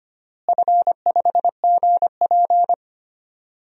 Morse code, heard as a keyed tone, about 25 wpm, 700 Hz. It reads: F5GP
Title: DXCC-100 Call Signs - 4 Characters - Repeated 2x - 25wpm